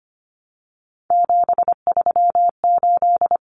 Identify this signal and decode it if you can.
Morse code, heard as a keyed tone, about 25 wpm, 700 Hz. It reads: 738
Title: Numbers - 3-Digit 25wpm